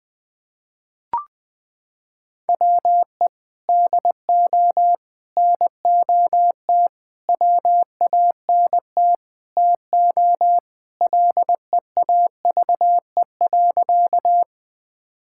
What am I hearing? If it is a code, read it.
Morse code, heard as a keyed tone, about 19 wpm, 700 Hz. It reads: WE DO NOT WANT TO LEAVE.
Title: Sentences with 5-Word Phrases - 20wpm